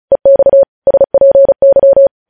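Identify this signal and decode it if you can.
Morse code, heard as a keyed tone, about 36 wpm, 550 Hz. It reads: EX SPY
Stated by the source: Morse code news headlines sent at 35, beginning with vvv.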